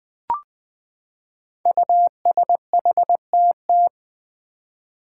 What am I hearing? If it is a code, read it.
Morse code, heard as a keyed tone, about 20 wpm, 700 Hz. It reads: USHTT
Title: Five Challenging Letter-Number Combinations 20wpm